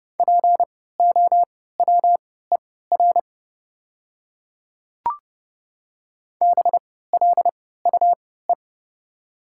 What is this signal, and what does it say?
Morse code, heard as a keyed tone, about 30 wpm, spaced as for 17 wpm, 690 Hz. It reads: POWER BLUE
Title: 3x Character Spacing - Top 500 Words - 30wpm